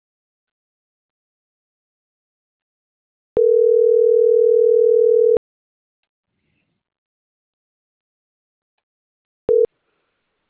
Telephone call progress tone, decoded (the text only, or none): ringback tone